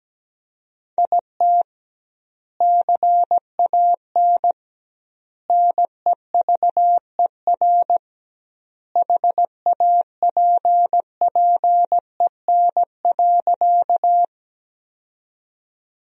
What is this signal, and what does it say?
Morse code, heard as a keyed tone, about 17 wpm, 700 Hz. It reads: IT CAN NEVER HAPPEN.
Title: 2x Word-Spacing - Sentences from Top 400 Words - 17wpm